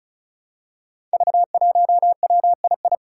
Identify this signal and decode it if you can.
Morse code, heard as a keyed tone, about 36 wpm, 700 Hz. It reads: V1WII
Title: Five Challenging Letter-Number Combinations 35wpm